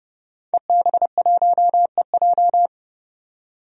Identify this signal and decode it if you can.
Morse code, heard as a keyed tone, about 30 wpm, 700 Hz. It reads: EB1EJ
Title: DXCC-100 Call Signs - 5 Characters - 30wpm